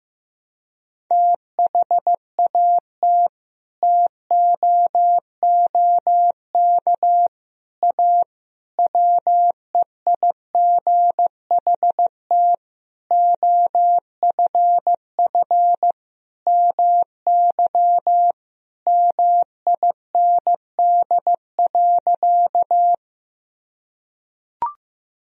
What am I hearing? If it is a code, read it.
Morse code, heard as a keyed tone, about 15 wpm, 700 Hz. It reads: THAT TOOK A WEIGHT OFF MY MIND.
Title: Sentences from Top 1000 Words - Review Part 3 - 15wpm